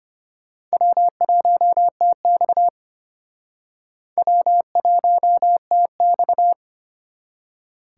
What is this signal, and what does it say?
Morse code, heard as a keyed tone, about 33 wpm, 700 Hz. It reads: W1TX W1TX